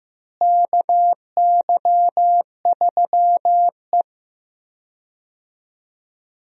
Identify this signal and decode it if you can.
Morse code, heard as a keyed tone, about 16 wpm, 700 Hz. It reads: KY3E
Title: USA Call Signs - 4 Characters - Repeated 2x - 15wpm